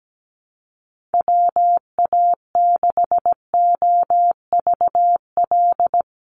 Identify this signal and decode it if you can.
Morse code, heard as a keyed tone, about 17 wpm, 700 Hz. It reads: WA6OVL
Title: USA Call Signs Repeated 2x 17wpm